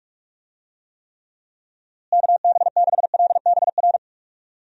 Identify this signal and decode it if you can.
Morse code, heard as a keyed tone, about 42 wpm, 700 Hz. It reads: KB6LBR
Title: USA Call Signs - 6 Characters - Repeated 2x - 45wpm